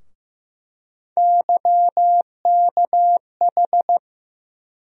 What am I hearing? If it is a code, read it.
Morse code, heard as a keyed tone, about 15 wpm, 700 Hz. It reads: YKH
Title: USA Call Sign Postfixes - 3 Letters 15wpm